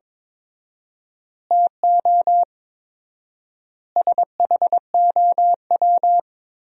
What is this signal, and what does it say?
Morse code, heard as a keyed tone, about 22 wpm, 700 Hz. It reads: TO SHOW
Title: 4x Word-Spacing - Sentences from Top 200 Words - 22wpm